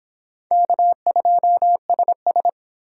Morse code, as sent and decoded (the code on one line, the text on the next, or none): -.- ..--- ... ...
K2SS